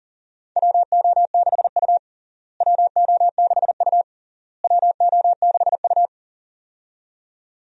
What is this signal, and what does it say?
Morse code, heard as a keyed tone, about 37 wpm, 700 Hz. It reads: WO6U WO6U WO6U
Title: USA Call Signs - 4 Characters - Repeated 3x - 40wpm